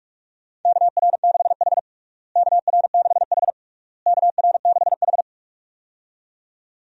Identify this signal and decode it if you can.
Morse code, heard as a keyed tone, about 46 wpm, 700 Hz. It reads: KR6H KR6H KR6H
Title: USA Call Signs - 4 Characters - Repeated 3x - 45wpm